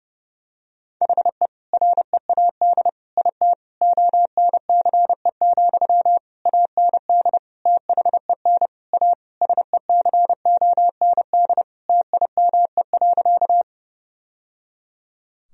Morse code, heard as a keyed tone, about 32 wpm, 700 Hz. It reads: HE READ IT ONCE, AND THEN A SECOND TIME.
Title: Sentences from Top 1000 Words - Review Part 1 - 30wpm